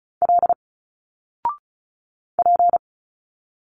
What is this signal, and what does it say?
Morse code, heard as a keyed tone, about 34 wpm, 700 Hz. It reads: L P
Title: Single Letters 35wpm